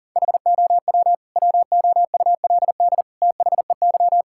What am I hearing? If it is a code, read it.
Morse code, heard as a keyed tone, about 40 wpm, 700 Hz. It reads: HOW WOULD THEY
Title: Sentences from Top 1000 Words - Review Part 6 - 40wpm